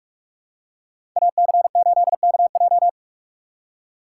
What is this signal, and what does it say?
Morse code, heard as a keyed tone, about 44 wpm, 690 Hz. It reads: AC8KJ
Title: USA Call Signs - 5 Characters - 6x Character Spacing - 45wpm